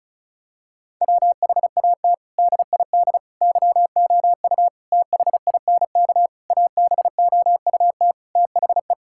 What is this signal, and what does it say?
Morse code, heard as a keyed tone, about 34 wpm, 700 Hz. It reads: WHAT DID YOU THINK ABOUT THE
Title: Sentences from Top 1000 Words - Review Part 5 - 35wpm